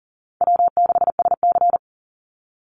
Morse code, heard as a keyed tone, about 40 wpm, 700 Hz. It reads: W6SC
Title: Easy Call Signs Repeated 3x - 40wpm